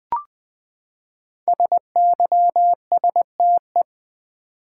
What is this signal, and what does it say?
Morse code, heard as a keyed tone, about 20 wpm, 700 Hz. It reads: SYSTE
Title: Build A Word - 5 Letters At A Time - 10-Letter Words - 20wpm